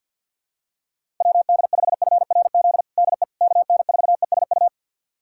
Morse code, heard as a keyed tone, about 52 wpm, 700 Hz. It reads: WD5FRZ DE KN4ESU
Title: USA Callsign DE Callsign 50wpm